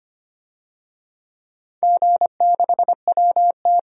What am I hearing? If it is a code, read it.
Morse code, heard as a keyed tone, about 25 wpm, 700 Hz. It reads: G6WT
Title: DXCC-100 Call Signs - 4 Characters - 25wpm